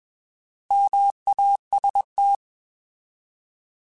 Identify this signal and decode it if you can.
Morse code, heard as a keyed tone, about 21 wpm, 800 Hz. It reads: MAST
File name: Alphabetical / Alphabetical mp3s / Ms @20